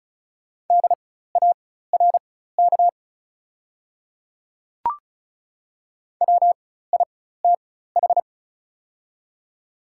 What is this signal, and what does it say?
Morse code, heard as a keyed tone, about 34 wpm, spaced as for 16 wpm, 690 Hz. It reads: DARK WITH
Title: 4x Character Spacing - Top 500 Words - 35wpm